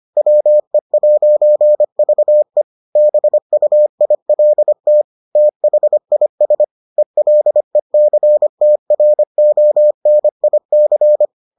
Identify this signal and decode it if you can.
Morse code, heard as a keyed tone, about 25 wpm, 600 Hz. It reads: WE'VE BUILT THIS ELECTRONIC